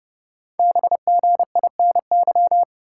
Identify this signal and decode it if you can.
Morse code, heard as a keyed tone, about 30 wpm, 700 Hz. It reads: BGINY
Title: Code Groups - 5 Letters - 30wpm